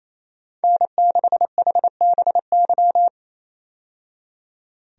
Morse code, heard as a keyed tone, about 28 wpm, 700 Hz. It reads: N6HBY
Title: USA Call Signs - 5 Characters - 28wpm